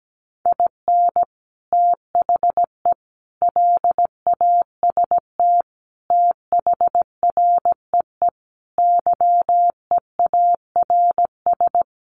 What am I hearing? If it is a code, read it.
Morse code, heard as a keyed tone, about 17 wpm, 700 Hz. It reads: IN THE LAST THREE YEARS
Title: Sets of 5 Words Encore2 17wpm